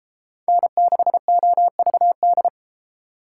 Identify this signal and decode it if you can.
Morse code, heard as a keyed tone, about 31 wpm, 700 Hz. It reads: N6OVD